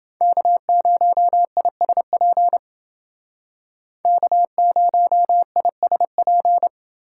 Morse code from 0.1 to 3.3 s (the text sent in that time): K0ISP